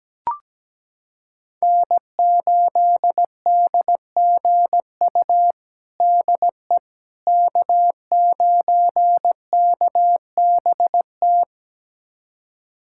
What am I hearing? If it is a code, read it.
Morse code, heard as a keyed tone, about 17 wpm, 700 Hz. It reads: N8DGU DE K9KBT
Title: USA Callsign DE Callsign - 5 Characters - 17wpm